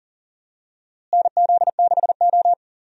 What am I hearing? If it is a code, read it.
Morse code, heard as a keyed tone, about 38 wpm, 700 Hz. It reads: NZ6O
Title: USA Call Signs - 4 Characters - Repeated 3x - 40wpm